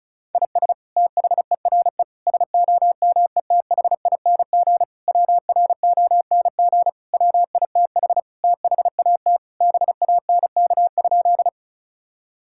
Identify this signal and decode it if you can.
Morse code, heard as a keyed tone, about 36 wpm, 700 Hz. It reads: IS THERE SOMETHING WRONG WITH THAT BANK?
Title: Sentences from Top 1000 Words - Review Part 6 - 35wpm